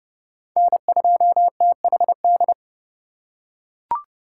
Morse code, heard as a keyed tone, about 31 wpm, 700 Hz. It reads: N2THD